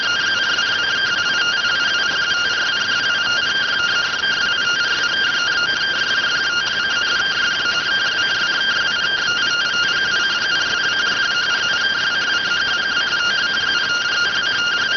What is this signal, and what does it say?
Two-tone teleprinter signal, about 170 Hz shift.